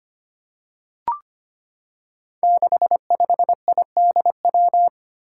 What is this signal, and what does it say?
Morse code, heard as a keyed tone, about 25 wpm, 700 Hz. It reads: 65IDW
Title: Five Challenging Letter-Number Combinations 25wpm